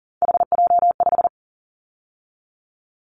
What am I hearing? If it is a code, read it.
Morse code, heard as a keyed tone, about 39 wpm, 700 Hz. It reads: HJ5
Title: Callsign Prefixes for Top 100 DXCC - 40wpm